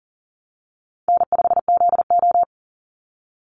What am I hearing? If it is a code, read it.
Morse code, heard as a keyed tone, about 40 wpm, 700 Hz. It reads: N5ZO
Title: Easy Call Signs Repeated 2x - 40wpm